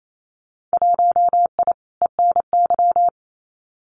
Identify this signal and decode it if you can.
Morse code, heard as a keyed tone, about 28 wpm, 700 Hz. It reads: 1I ENY
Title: Winter Field Day Exchanges Encore 28wpm